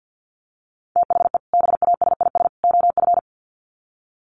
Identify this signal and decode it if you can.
Morse code, heard as a keyed tone, about 50 wpm, 700 Hz. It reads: THE BASIS OF
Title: Sets of 3 Words 50wpm